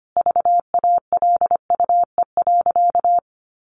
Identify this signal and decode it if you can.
Morse code, heard as a keyed tone, about 25 wpm, 700 Hz. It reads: VALUE.
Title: Sentences from Top 1400 Words 25wpm